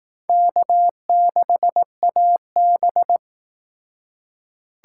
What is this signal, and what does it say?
Morse code, heard as a keyed tone, about 18 wpm, 700 Hz. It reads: K6AB
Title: Speed Racing - USA Call Signs 15wpm